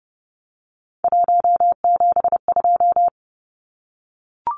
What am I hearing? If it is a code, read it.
Morse code, heard as a keyed tone, about 30 wpm, 700 Hz. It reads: 172
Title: Numbers - 3-Digit 30wpm